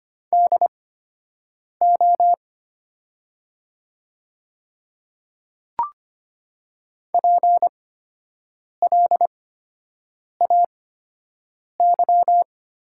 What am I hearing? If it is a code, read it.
Morse code, heard as a keyed tone, about 25 wpm, spaced as for 7 wpm, 700 Hz. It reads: DO PLAY
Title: 8x Character Spacing - Top 200 Words - 25wpm